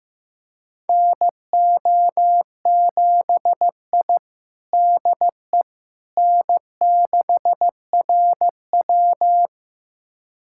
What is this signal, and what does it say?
Morse code, heard as a keyed tone, about 15 wpm, 700 Hz. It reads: NO7I DE N6RW
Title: USA Callsign DE Callsign - 4 Characters - 15wpm